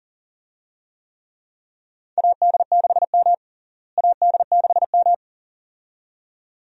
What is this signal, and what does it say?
Morse code, heard as a keyed tone, about 40 wpm, 700 Hz. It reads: AD6M AD6M
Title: USA Call Signs - 4 Characters - Repeated 2x - 40wpm